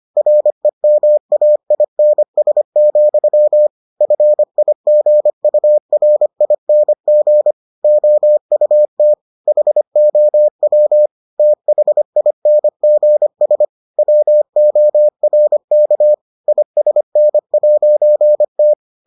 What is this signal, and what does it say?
Morse code, heard as a keyed tone, about 25 wpm, 600 Hz. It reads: REMAINS, FIGURING OUT HOW THINGS WORK ISN'T